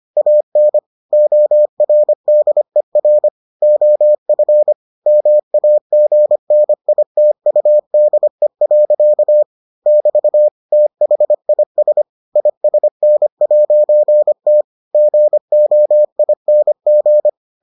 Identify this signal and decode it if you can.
Morse code, heard as a keyed tone, about 25 wpm, 600 Hz. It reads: AN ORDER OF MAGNITUDE. <BT> THIS ISN'T GOING